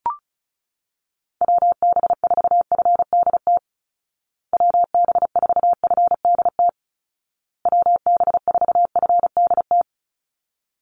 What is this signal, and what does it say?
Morse code, heard as a keyed tone, about 34 wpm, 700 Hz. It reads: WB4FDT WB4FDT WB4FDT
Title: Call Signs Repeated 2x - 35wpm